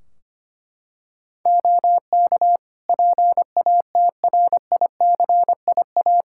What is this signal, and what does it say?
Morse code, heard as a keyed tone, about 25 wpm, 700 Hz. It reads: OK PATRICIA
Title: Practice QSO - Exchange 2 - Info - 25wpm